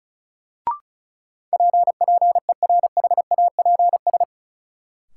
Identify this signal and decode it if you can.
Morse code, heard as a keyed tone, about 36 wpm, 700 Hz. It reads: P PERHAPS
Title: Sentences from Top 1100 Words - New Word Drill Alt - 35wpm